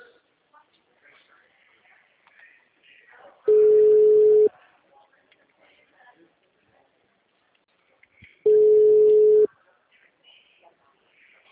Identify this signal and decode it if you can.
Ringback tone (Europe, 425 Hz), with speech.